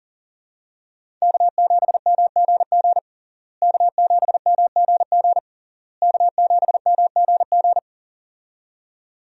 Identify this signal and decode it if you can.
Morse code, heard as a keyed tone, about 40 wpm, 700 Hz. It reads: K7MGG K7MGG K7MGG
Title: USA Call Signs - 5 Characters - Repeated 3x - 40wpm